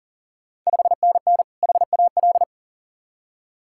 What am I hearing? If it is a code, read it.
Morse code, heard as a keyed tone, about 40 wpm, 690 Hz. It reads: 5NN HAL
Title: Ontario QSO Party Exchange 40wpm